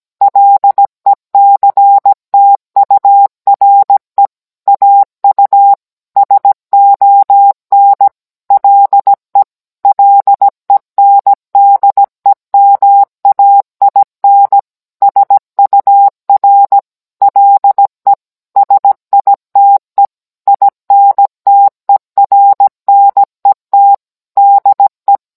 Morse code, heard as a keyed tone, about 17 wpm, 800 Hz. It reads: LECTURE AU SON LE LENDEMAIN SUR LE SITE INTERNET DE